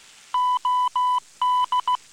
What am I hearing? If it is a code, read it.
Morse code, heard as a keyed tone, about 16 wpm, 1000 Hz. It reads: OD